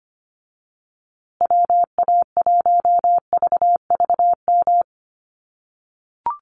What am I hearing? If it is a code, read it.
Morse code, heard as a keyed tone, about 25 wpm, 700 Hz. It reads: WA1VVM
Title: USA Call Signs Repeated 2x 25wpm